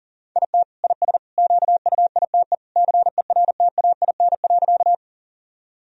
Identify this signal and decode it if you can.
Morse code, heard as a keyed tone, about 40 wpm, 700 Hz. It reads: IT IS QUITE CERTAIN.
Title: Sentences from Top 1000 Words - Review Part 2 - 40wpm